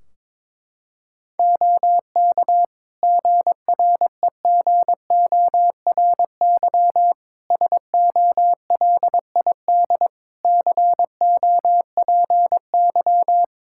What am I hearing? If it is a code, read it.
Morse code, heard as a keyed tone, about 21 wpm, 700 Hz. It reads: OK GREGORY SOLID COPY
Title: Practice QSO - Exchange 3 - More Info - 22wpm